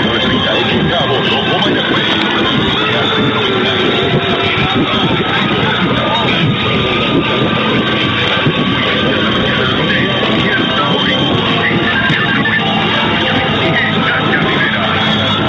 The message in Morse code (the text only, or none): none